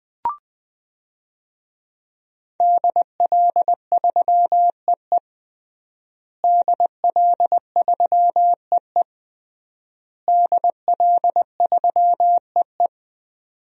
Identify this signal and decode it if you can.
Morse code, heard as a keyed tone, about 20 wpm, 700 Hz. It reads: DL3EE DL3EE DL3EE
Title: DXCC-100 Call Signs - 5 Characters - Repeated 3x - 20wpm